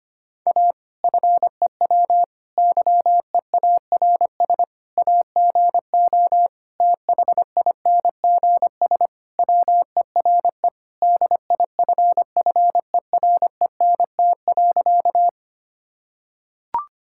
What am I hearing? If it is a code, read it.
Morse code, heard as a keyed tone, about 25 wpm, 700 Hz. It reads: A FEW YEARS AGO THINGS WERE DIFFERENT.
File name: practice / 25wpm / Sentences with 4-Word Phrases - 25wpm